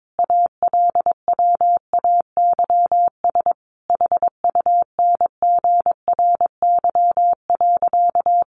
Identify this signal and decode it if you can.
Morse code, heard as a keyed tone, about 22 wpm, 700 Hz. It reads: ALWAYS HUNGRY.
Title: Sentences from Top 1400 Words 22wpm